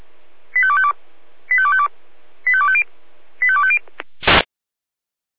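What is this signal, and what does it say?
Five-tone selective call (ZVEI), heard as quick runs of five tones, tones of about 70 ms. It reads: 85252 85252 85250 85250
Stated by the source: ZVEI five-tone selective call received on SDR.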